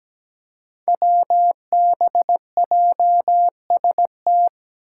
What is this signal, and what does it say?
Morse code, heard as a keyed tone, about 17 wpm, 700 Hz. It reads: WBJST
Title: Five Challenging Letter-Number Combinations 17wpm